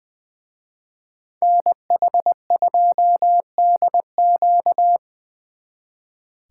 Morse code, heard as a keyed tone, about 24 wpm, 700 Hz. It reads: NH2DQ
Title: Speed Racing - USA Call Signs 20wpm